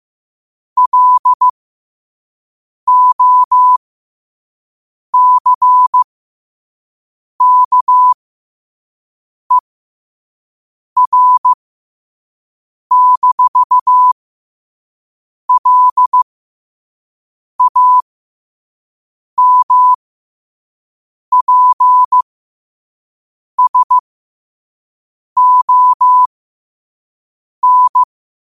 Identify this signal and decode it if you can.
Morse code, heard as a keyed tone, about 15 wpm, spaced as for 5 wpm, 1000 Hz. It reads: LOCKER-LAMPSON
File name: qotdfiles / 5wpm